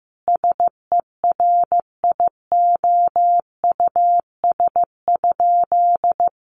Morse code, heard as a keyed tone, about 15 wpm, 700 Hz. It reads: SERIOUS?
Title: Sentences from Top 1100 Words 15wpm